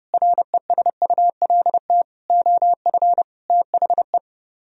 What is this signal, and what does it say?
Morse code, heard as a keyed tone, about 36 wpm, 700 Hz. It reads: RESULT OF THE
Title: Speed-Racing - Sets of 5 Words 30wpm